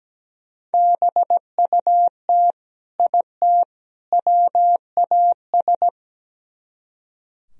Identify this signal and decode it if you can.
Morse code, heard as a keyed tone, about 19 wpm, 700 Hz. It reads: BUT IT WAS
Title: Speed-Racing - Sets of 3 Words 17wpm